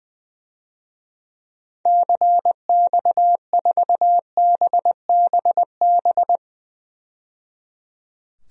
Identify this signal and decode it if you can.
Morse code, heard as a keyed tone, about 20 wpm, 700 Hz. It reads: CX4BBB
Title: DXCC-100 Call Signs - 6 Characters - 20wpm